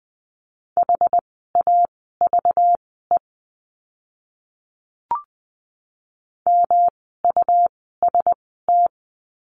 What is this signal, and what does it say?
Morse code, heard as a keyed tone, about 20 wpm, spaced as for 14 wpm, 700 Hz. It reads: HAVE MUST
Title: 237 Lesson 38f - Letter q - Words 20wpm